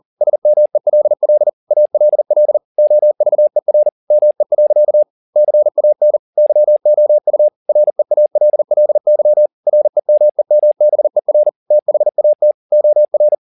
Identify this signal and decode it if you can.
Morse code, heard as a keyed tone, about 40 wpm, 600 Hz. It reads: SMELL ALL OVER ME. CAN YOU REALLY REMEMBER THAT OR
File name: the old man and the sea 1 40wpm